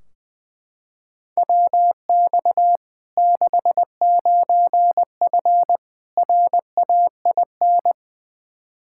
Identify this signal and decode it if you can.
Morse code, heard as a keyed tone, about 20 wpm, 700 Hz. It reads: WX 69F RAIN
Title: Practice QSO - Exchange 2 - Info - 20wpm